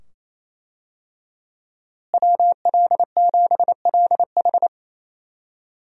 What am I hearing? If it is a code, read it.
Morse code, heard as a keyed tone, about 27 wpm, 700 Hz. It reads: WL7LH
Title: USA Call Signs - 5 Characters - Repeated 2x - 28wpm